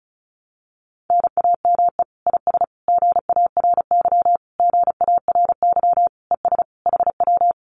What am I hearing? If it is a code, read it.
Morse code, heard as a keyed tone, about 34 wpm, 700 Hz. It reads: NAME IS GARY GARY ES HW
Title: Common QSO Phrases Encore 35wpm